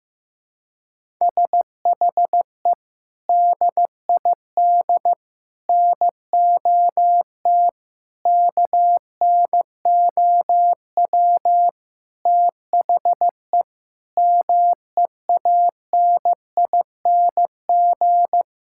Morse code, heard as a keyed tone, about 16 wpm, 700 Hz. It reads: SHE DID NOT KNOW THE MEANING
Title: Sentences from Top 1000 Words - Review Part 6 - 15wpm